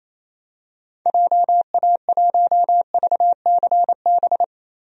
Morse code, heard as a keyed tone, about 28 wpm, 700 Hz. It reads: JA1VCB
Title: DXCC-100 Call Signs - 6 Characters - 28wpm